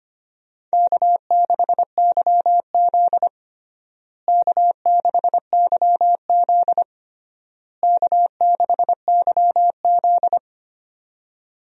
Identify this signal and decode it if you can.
Morse code, heard as a keyed tone, about 25 wpm, 700 Hz. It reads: K6YZ K6YZ K6YZ
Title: USA Call Signs - 4 Characters - Repeated 3x - 25wpm